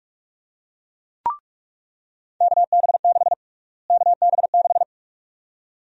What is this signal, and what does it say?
Morse code, heard as a keyed tone, about 46 wpm, 700 Hz. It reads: KB6 KB6
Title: USA Call Sign Prefixes - Repeated 2x - 45wpm